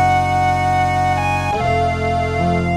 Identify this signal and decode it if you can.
There is no code.